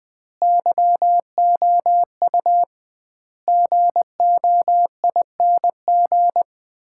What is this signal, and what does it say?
Morse code, heard as a keyed tone, about 20 wpm, 700 Hz. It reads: YOU GOING
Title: 2x Word-Spacing - Sets of 5 Words - 20wpm